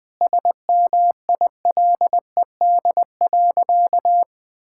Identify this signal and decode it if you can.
Morse code, heard as a keyed tone, about 20 wpm, 700 Hz. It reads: SMILED.
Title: Sentences with 5-Word Phrases - 20wpm